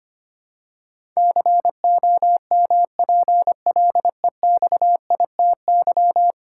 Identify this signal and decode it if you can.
Morse code, heard as a keyed tone, about 25 wpm, 700 Hz. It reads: COMPLEXITY
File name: Ten Letter Words - 25wpm